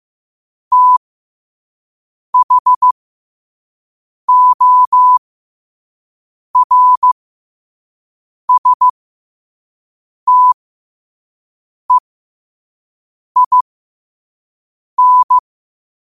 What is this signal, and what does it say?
Morse code, heard as a keyed tone, about 15 wpm, spaced as for 5 wpm, 1000 Hz. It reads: THORSTEIN
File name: qotdfiles / 5wpm